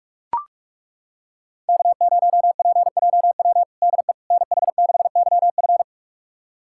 Morse code, heard as a keyed tone, about 41 wpm, 700 Hz. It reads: K0PJW DE NH6YF
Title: USA Callsign DE Callsign - 5 Characters - 45wpm